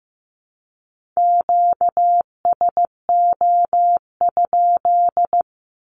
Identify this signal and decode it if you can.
Morse code, heard as a keyed tone, about 15 wpm, 700 Hz. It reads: QSO?
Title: Q-Codes 15wpm